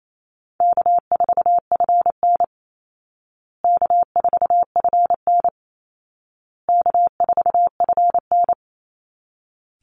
Morse code, heard as a keyed tone, about 28 wpm, 700 Hz. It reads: K4FN K4FN K4FN
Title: Easy Call Signs Repeated 3x - 28wpm